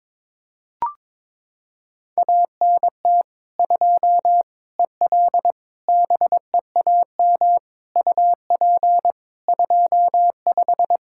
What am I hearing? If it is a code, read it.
Morse code, heard as a keyed tone, about 22 wpm, 700 Hz. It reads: ANT 2 EL BEAM UP 25
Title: Practice QSO - Exchange 2 - Info - 22wpm